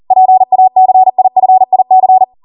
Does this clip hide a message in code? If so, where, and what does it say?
Yes: Morse code from 0.1 to 2.2 s, heard as a keyed tone, about 40 wpm, 750 Hz. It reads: PACIFIC